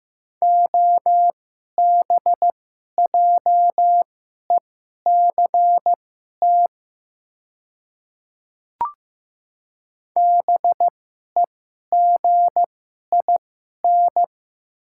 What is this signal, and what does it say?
Morse code, heard as a keyed tone, about 15 wpm, spaced as for 11 wpm, 700 Hz. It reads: OBJECT BEGIN
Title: 2x Character Spacing - Top 500 Words - 15wpm